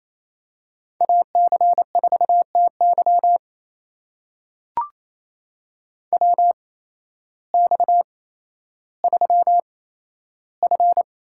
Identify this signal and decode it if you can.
Morse code, heard as a keyed tone, about 28 wpm, spaced as for 13 wpm, 700 Hz. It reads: AC4TY WX3F
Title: USA Call Signs - 8x Character Spacing - 28wpm